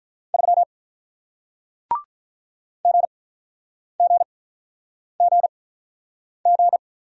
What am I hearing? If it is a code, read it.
Morse code, heard as a keyed tone, about 48 wpm, spaced as for 9 wpm, 700 Hz. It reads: 3 GGGG